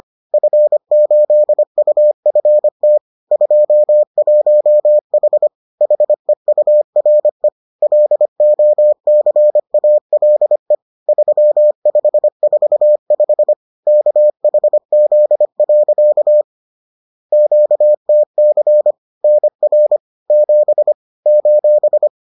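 Morse code, heard as a keyed tone, about 25 wpm, 600 Hz. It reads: F8UFT 21H HEURE LOCALE 3545 KHZ. QTC NR 7 :
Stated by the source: Bulletin sent in CW.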